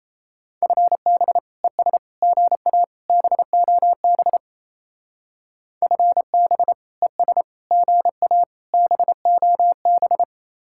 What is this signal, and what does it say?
Morse code, heard as a keyed tone, about 34 wpm, 700 Hz. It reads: FB ES GA BOB FB ES GA BOB
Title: Speed-Racing Common QSO Phrases - 28wpm